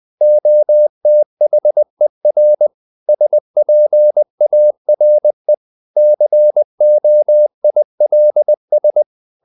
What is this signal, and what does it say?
Morse code, heard as a keyed tone, about 20 wpm, 600 Hz. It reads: OTHER SPARE COILS
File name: the old man and the sea / the old man and the sea part 1 20wpm